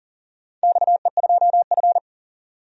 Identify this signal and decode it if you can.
Morse code, heard as a keyed tone, about 40 wpm, 700 Hz. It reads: XE2F
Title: DXCC-100 Call Signs - 4 Characters - 40wpm